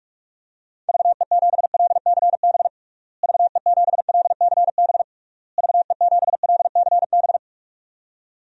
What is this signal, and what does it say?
Morse code, heard as a keyed tone, about 46 wpm, 700 Hz. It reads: VE7LCB VE7LCB VE7LCB
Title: DXCC-100 Call Signs - 6 Characters - Repeated 3x - 45wpm